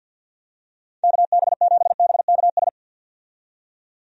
Morse code, heard as a keyed tone, about 46 wpm, 700 Hz. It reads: KB7BKS
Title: USA Call Signs - 6 Characters - 50wpm